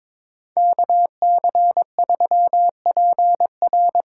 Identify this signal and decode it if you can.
Morse code, heard as a keyed tone, about 22 wpm, 700 Hz. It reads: KC3PR
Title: USA Call Signs - 5 Characters - 2x Character Spacing - 22wpm